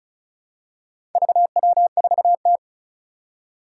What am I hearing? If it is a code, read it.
Morse code, heard as a keyed tone, about 36 wpm, 700 Hz. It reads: VW4T
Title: Four Challenging Letter-Number Combinations 35wpm